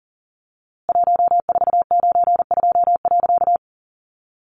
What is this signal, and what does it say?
Morse code, heard as a keyed tone, about 40 wpm, 700 Hz. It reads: 1492.
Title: A Sound Of Thunder 40wpm